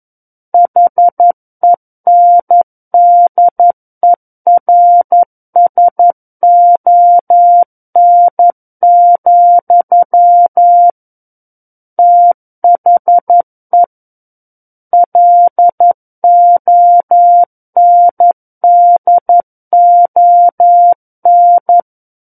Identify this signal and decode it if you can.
Morse code, heard as a keyed tone, about 11 wpm, 700 Hz. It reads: HENDERSON, THE LONDON
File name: Book One / 02-Chapter 2 - 11 WPM